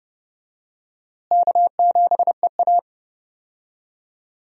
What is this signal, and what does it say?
Morse code, heard as a keyed tone, about 30 wpm, 700 Hz. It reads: K7EA
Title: USA Call Signs - 4 Characters - Repeated 3x - 30wpm